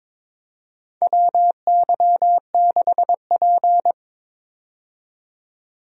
Morse code, heard as a keyed tone, about 22 wpm, 700 Hz. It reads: WY6P
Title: USA Call Signs - 4 Characters - 22wpm